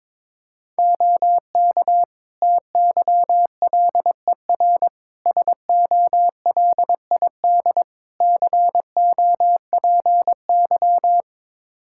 Morse code, heard as a keyed tone, about 22 wpm, 700 Hz. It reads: OK TYLER SOLID COPY
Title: Practice QSO - Exchange 3 - More Info - 22wpm